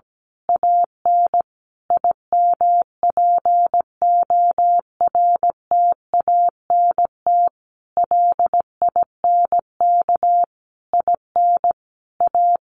Morse code, heard as a keyed tone, about 17 wpm, 700 Hz. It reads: AN IMPORTANT LINK IN A
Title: A Sound Of Thunder 17wpm